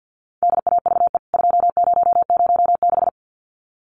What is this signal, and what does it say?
Morse code, heard as a keyed tone, about 49 wpm, 700 Hz. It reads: DAVE 2006
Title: CWT Exchange 50wpm